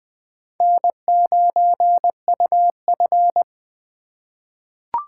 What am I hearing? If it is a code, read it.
Morse code, heard as a keyed tone, about 20 wpm, 700 Hz. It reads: N9UF
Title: USA Call Signs - 4 Characters - 2x Character Spacing - 20wpm